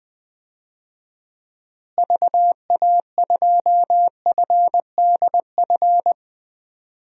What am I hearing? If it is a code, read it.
Morse code, heard as a keyed tone, about 20 wpm, 700 Hz. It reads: VA2FDF
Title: DXCC-100 Call Signs - 6 Characters - Repeated 2x - 20wpm